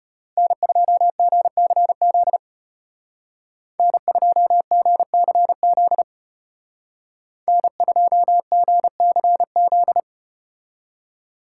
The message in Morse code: N2GCZ N2GCZ N2GCZ

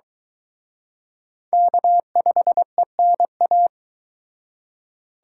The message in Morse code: K5ENA

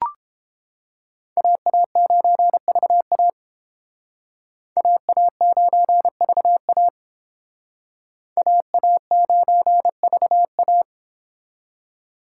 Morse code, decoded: AA9VA AA9VA AA9VA

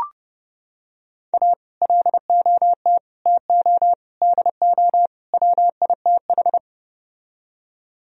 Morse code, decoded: A LOT TO DO WITH

